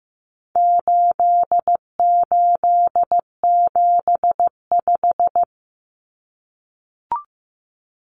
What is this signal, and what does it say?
Morse code, heard as a keyed tone, about 15 wpm, 700 Hz. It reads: 8875
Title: Common Cut Numbers - 4-Digit 15wpm